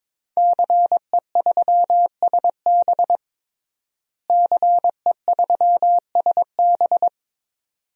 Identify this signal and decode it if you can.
Morse code, heard as a keyed tone, about 22 wpm, 700 Hz. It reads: CE3SB CE3SB
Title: DXCC-100 Call Signs - 5 Characters - Repeated 2x - 22wpm